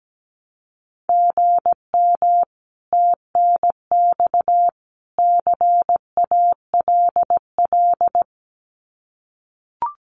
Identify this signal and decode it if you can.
Morse code, heard as a keyed tone, about 17 wpm, 700 Hz. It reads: GM TNX CALL
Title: Common QSO Phrases 17wpm